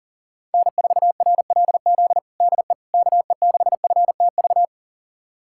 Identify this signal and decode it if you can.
Morse code, heard as a keyed tone, about 40 wpm, 700 Hz. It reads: N4RLZ DE KE6FTV
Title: USA Callsign DE Callsign 40wpm